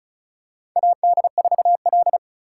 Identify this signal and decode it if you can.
Morse code, heard as a keyed tone, about 36 wpm, 700 Hz. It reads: AD4L
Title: USA Call Signs - 4 Characters - 3x Character Spacing - 35wpm